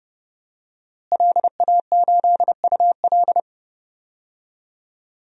Morse code, heard as a keyed tone, about 32 wpm, 700 Hz. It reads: LA8UL
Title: DXCC-100 Call Signs - 5 Characters - Repeated 2x - 30wpm